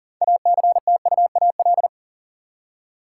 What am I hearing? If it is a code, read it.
Morse code, heard as a keyed tone, about 40 wpm, 700 Hz. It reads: ACTUAL
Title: Six Letter Words - 40wpm